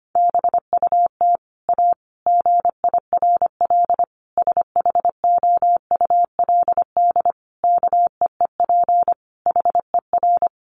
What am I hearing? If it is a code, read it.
Morse code, heard as a keyed tone, about 25 wpm, 700 Hz. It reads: BUT A GIRL SHOULD KEEP HER
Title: Sentences from Top 1200 Words 25wpm